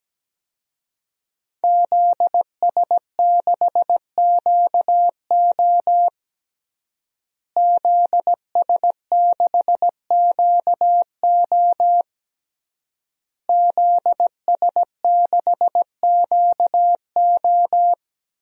Morse code, heard as a keyed tone, about 17 wpm, 700 Hz. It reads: ZS6QO ZS6QO ZS6QO
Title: DXCC-100 Call Signs - 5 Characters - Repeated 3x - 17wpm